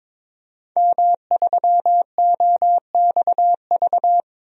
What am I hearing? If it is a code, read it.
Morse code, heard as a keyed tone, about 22 wpm, 700 Hz. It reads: M3OXV
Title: DXCC-100 Call Signs - 5 Characters - Repeated 3x - 22wpm